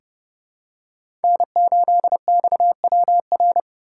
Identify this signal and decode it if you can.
Morse code, heard as a keyed tone, about 30 wpm, 700 Hz. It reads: N8XWR
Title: USA Call Signs - 2x Character Spacing - 30wpm